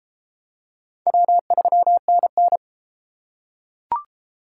Morse code, heard as a keyed tone, about 30 wpm, 700 Hz. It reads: W3NN